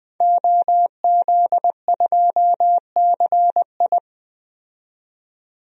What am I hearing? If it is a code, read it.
Morse code, heard as a keyed tone, about 20 wpm, 700 Hz. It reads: OZ2CI
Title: DXCC-100 Call Signs - 5 Characters - 20wpm